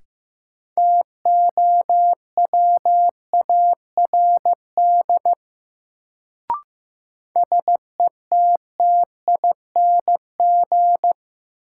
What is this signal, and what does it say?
Morse code, heard as a keyed tone, about 15 wpm, 700 Hz. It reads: TOWARD SETTING
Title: Sentences from Top 1300 Words - New Word Drill Alt - 15wpm